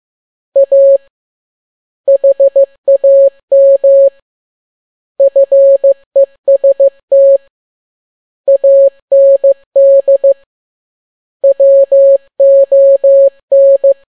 Morse code, heard as a keyed tone, about 15 wpm, 550 Hz. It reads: A HAM FEST AND WON
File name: n6tt p1 15C13